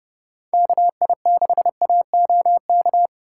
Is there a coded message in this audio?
Yes: Morse code from 0.5 to 3.1 s, heard as a keyed tone, about 29 wpm, 700 Hz. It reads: KI6AOK